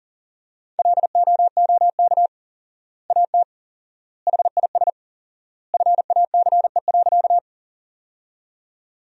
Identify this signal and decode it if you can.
Morse code, heard as a keyed tone, about 40 wpm, 700 Hz. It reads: LOOK AT HIS FACE.
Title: 4x Word-Spacing - Sentences from Top 300 Words - 40wpm